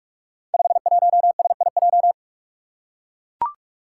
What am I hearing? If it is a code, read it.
Morse code, heard as a keyed tone, about 44 wpm, 700 Hz. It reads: 51SIJ
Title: Five Challenging Letter-Number Combinations 45wpm